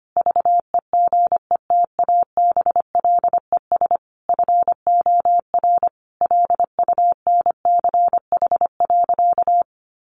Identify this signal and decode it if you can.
Morse code, heard as a keyed tone, about 25 wpm, 700 Hz. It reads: VEGETABLES FOR LUNCH.
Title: Easy Phrases 25wpm